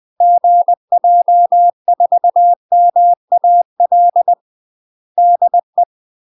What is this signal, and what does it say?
Morse code, heard as a keyed tone, about 20 wpm, 700 Hz. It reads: GJ4MAL DE
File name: UK MASTER QSOs TRAINING/20WPM